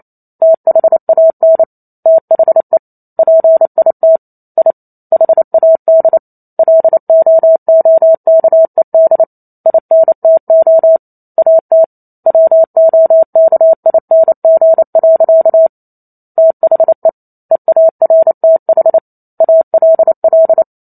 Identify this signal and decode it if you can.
Morse code, heard as a keyed tone, about 28 wpm, 650 Hz. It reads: THAN THE PIT I HAD LOOKED INTO AT WOKING. THE EARTH ALL